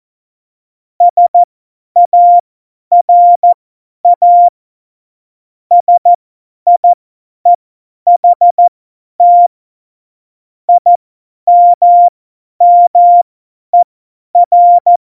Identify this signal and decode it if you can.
Morse code, heard as a keyed tone, about 14 wpm, spaced as for 10 wpm, 700 Hz. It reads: SARA SIEHT IMMER
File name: Morsekurs / Witze / Witz 03. Oktober 17, Tempo 50 F70